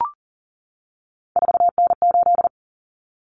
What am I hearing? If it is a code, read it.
Morse code, heard as a keyed tone, about 40 wpm, 700 Hz. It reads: 4N8
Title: Common Cut Numbers - 3-Digit 40wpm